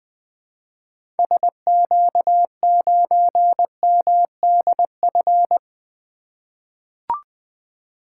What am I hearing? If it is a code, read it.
Morse code, heard as a keyed tone, about 21 wpm, 700 Hz. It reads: SQ9MDF
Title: 2023 SOTA Activator Call Signs - 20wpm